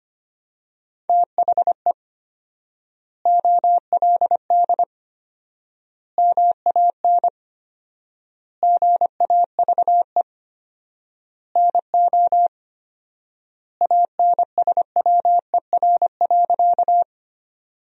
Morse code, heard as a keyed tone, about 25 wpm, 700 Hz. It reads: THE OLD MAN GAVE NO ANSWER.
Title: 4x Word-Spacing - Sentences from Top 500 Words - 25wpm